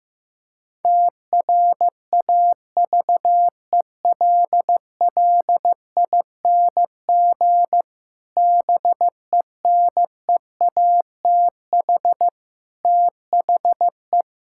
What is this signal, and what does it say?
Morse code, heard as a keyed tone, about 15 wpm, 700 Hz. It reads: TRAVELLING BENEATH THE
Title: Pangram Challenge v2 - 15wpm